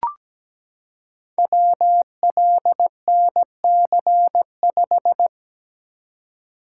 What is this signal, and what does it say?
Morse code, heard as a keyed tone, about 17 wpm, 700 Hz. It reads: WLNC5
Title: Code Groups - 5 Letter:Numbers - 17wpm